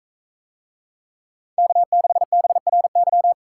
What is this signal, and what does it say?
Morse code, heard as a keyed tone, about 42 wpm, 700 Hz. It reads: K6BRY